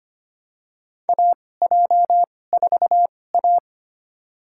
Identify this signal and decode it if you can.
Morse code, heard as a keyed tone, about 25 wpm, spaced as for 18 wpm, 700 Hz. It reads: AJ4A
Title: USA Call Signs - 4 Characters - 2x Character Spacing - 25wpm